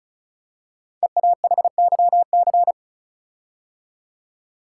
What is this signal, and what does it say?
Morse code, heard as a keyed tone, about 36 wpm, 700 Hz. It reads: EAHYC
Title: Code Groups - 5 Letters - 35wpm